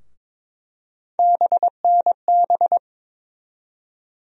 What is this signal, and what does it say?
Morse code, heard as a keyed tone, about 21 wpm, 700 Hz. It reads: BNB
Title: USA Call Sign Postfixes - 3 Letters 22wpm